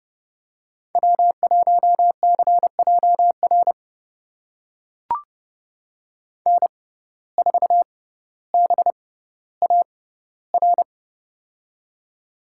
Morse code, heard as a keyed tone, about 30 wpm, spaced as for 15 wpm, 700 Hz. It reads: W1CJR N4BAR
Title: USA Call Signs - 6x Character Spacing - 30wpm